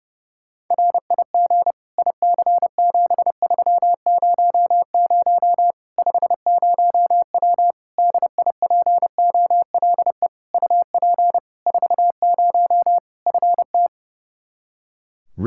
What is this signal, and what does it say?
Morse code, heard as a keyed tone, about 30 wpm, 700 Hz. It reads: RIG IC7300 50W DIPOLE UP 40 FT
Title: Practice QSO - Exchange 2 - Info - 30wpm